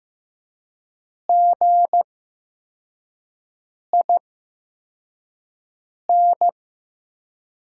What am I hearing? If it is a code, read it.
Morse code, heard as a keyed tone, about 15 wpm, spaced as for 4 wpm, 700 Hz. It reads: GIN
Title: USA Call Signs - 5 Characters - 8x Character Spacing - 15wpm